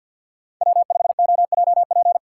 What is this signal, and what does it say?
Morse code, heard as a keyed tone, about 52 wpm, 700 Hz. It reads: W5OJP
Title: USA Call Signs - 5 Characters - 2x Character Spacing - 50wpm